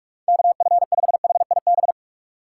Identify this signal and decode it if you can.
Morse code, heard as a keyed tone, about 46 wpm, 700 Hz. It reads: KF5HIB
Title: USA Call Signs - 6 Characters - 4x Character Spacing - 45wpm